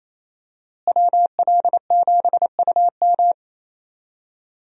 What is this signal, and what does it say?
Morse code, heard as a keyed tone, about 28 wpm, 700 Hz. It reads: WL7UM
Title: USA Call Signs - 5 Characters - Repeated 3x - 28wpm